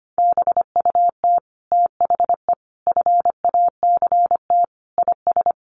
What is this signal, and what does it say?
Morse code, heard as a keyed tone, about 25 wpm, 700 Hz. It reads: BUT THE FACT IS
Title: Sets of 5 Words Encore2 25wpm